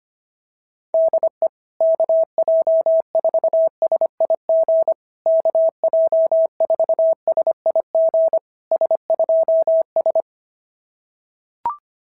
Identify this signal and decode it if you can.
Morse code, heard as a keyed tone, about 24 wpm, 650 Hz. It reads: DE KJ4SIG KJ4SIG S2S
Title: SOTA QSO Practice 25wpm